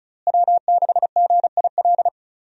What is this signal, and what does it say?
Morse code, heard as a keyed tone, about 34 wpm, 700 Hz. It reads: W6GIL
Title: USA Call Signs - 5 Characters - Repeated 2x - 35wpm